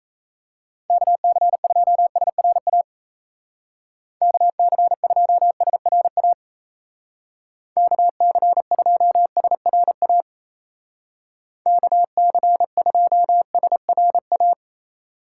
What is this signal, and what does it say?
Morse code, heard as a keyed tone, about 34 wpm, 700 Hz. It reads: KC2SRA KC2SRA KC2SRA KC2SRA
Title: Speed Racing - USA 2x3 Call Signs 28wpm